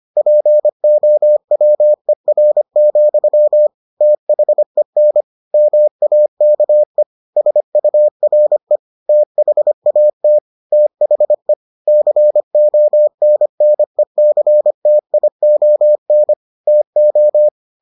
Morse code, heard as a keyed tone, about 25 wpm, 600 Hz. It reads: POWER, THEN MAKE SURE THAT THE CONNECTION TO